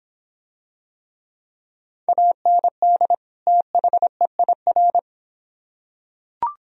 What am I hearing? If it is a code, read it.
Morse code, heard as a keyed tone, about 26 wpm, 700 Hz. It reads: AND THEIR